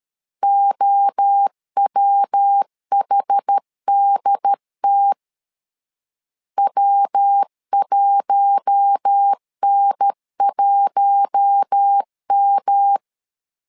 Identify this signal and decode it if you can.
Morse code, heard as a keyed tone, about 13 wpm, 800 Hz. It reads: OWHDT W1N1M
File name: LEZIONI / 4 Q1 12wpm 2028